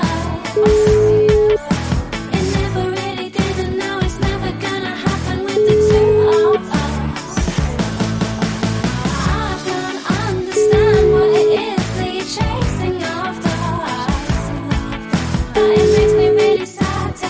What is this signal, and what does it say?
Ringback tone (Europe, 425 Hz), with noise.